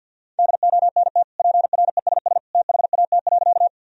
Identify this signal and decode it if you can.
Morse code, heard as a keyed tone, about 52 wpm, 700 Hz. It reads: DONT PRESS THAT.
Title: Sentences from Top 1000 Words - Review Part 1 - 50wpm